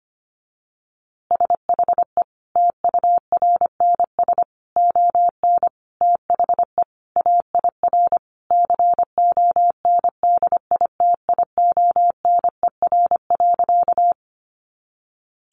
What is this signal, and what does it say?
Morse code, heard as a keyed tone, about 25 wpm, 700 Hz. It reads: SHE TURNS ON THE AIR CONDITIONER.
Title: Easy Phrases 25wpm